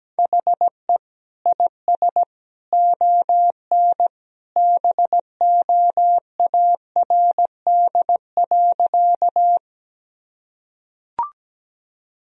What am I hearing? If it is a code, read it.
Morse code, heard as a keyed tone, about 17 wpm, 700 Hz. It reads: HE IS ON BOARD.
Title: Sentences from Top 1000 Words - Review Part 4 - 17wpm